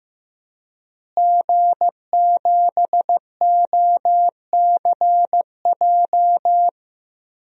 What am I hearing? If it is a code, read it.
Morse code, heard as a keyed tone, about 15 wpm, 700 Hz. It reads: G7OCJ
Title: DXCC-100 Call Signs - 5 Characters - Repeated 3x - 15wpm